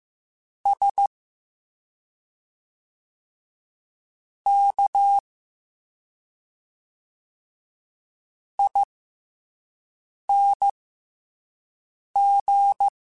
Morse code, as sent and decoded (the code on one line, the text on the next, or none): ... / -.- / .. -. --.
S K ING